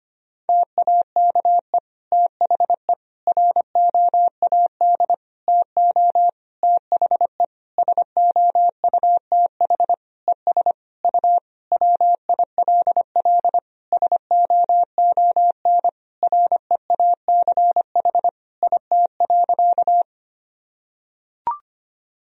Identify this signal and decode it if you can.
Morse code, heard as a keyed tone, about 25 wpm, 700 Hz. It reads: TAKE THE ROAD TO THE SOUTH ES U WILL SOON REACH IT.
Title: Sentences from Top 400 Words and Abbreviations 25wpm